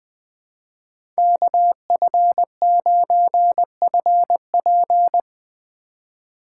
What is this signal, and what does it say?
Morse code, heard as a keyed tone, about 20 wpm, 700 Hz. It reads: KF9FP
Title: USA Call Signs - 5 Characters - 3x Character Spacing - 20wpm